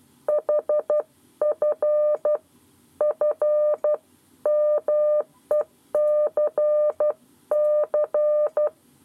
Morse code, heard as a keyed tone, about 12 wpm, 600 Hz. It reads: HF FMECC
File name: 7 Lezione cw 1